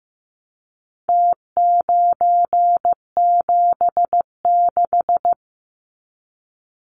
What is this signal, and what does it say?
Morse code, heard as a keyed tone, about 15 wpm, 700 Hz. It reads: T976
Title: Numbers - Contest Style - 4-Digit 15wpm